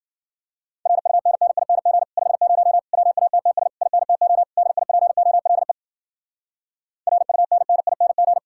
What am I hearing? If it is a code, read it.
Morse code, heard as a keyed tone, about 51 wpm, 700 Hz. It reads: RUNNING 50 WATTS INTO DIPOLE RUNNING